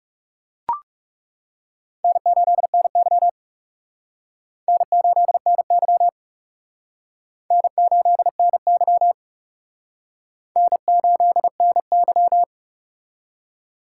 Morse code, heard as a keyed tone, about 37 wpm, 700 Hz. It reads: N8NY N8NY N8NY N8NY
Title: Speed Racing - USA 1x2 Call Signs 30wpm